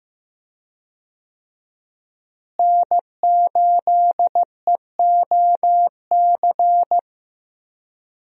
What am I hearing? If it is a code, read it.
Morse code, heard as a keyed tone, about 15 wpm, 700 Hz. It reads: N8EOC
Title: USA Call Signs - 5 Characters - Repeated 2x - 15wpm